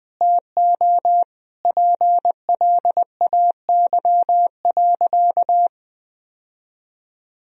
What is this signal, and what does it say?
Morse code, heard as a keyed tone, about 20 wpm, 700 Hz. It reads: TO PLAY.
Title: Sentences from Top 1000 Words - Review Part 4 - 20wpm